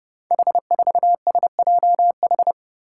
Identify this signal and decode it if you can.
Morse code, heard as a keyed tone, about 30 wpm, 700 Hz. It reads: H4SJH
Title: Five Challenging Letter-Number Combinations 30wpm